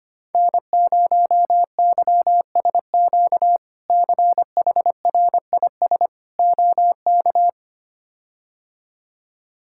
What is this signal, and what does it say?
Morse code, heard as a keyed tone, about 25 wpm, 700 Hz. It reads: N0YSQ CHRIS OK
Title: K1USN SST QSO Practice - 25wpm